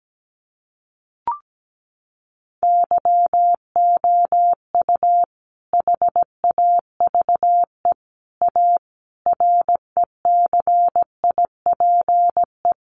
Morse code, heard as a keyed tone, about 17 wpm, 700 Hz. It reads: YOU HAVE A RECIPE
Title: Sentences from Top 1200 Words 17wpm